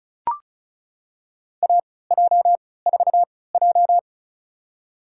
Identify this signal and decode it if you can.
Morse code, heard as a keyed tone, about 34 wpm, spaced as for 20 wpm, 700 Hz. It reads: AJ4J
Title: USA Call Signs - 4 Characters - 3x Character Spacing - 35wpm